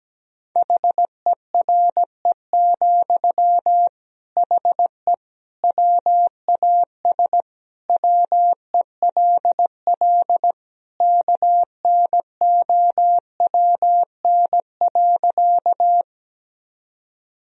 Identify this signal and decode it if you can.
Morse code, heard as a keyed tone, about 17 wpm, 700 Hz. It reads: HERE, HE WAS WELL KNOWN.
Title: Sentences from Top 1000 Words - Review Part 3 - 17wpm